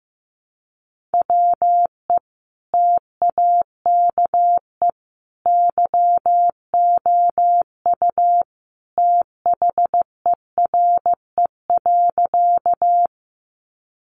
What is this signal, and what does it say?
Morse code, heard as a keyed tone, about 15 wpm, 700 Hz. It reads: WE TAKE YOU THERE.
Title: A Sound Of Thunder 15wpm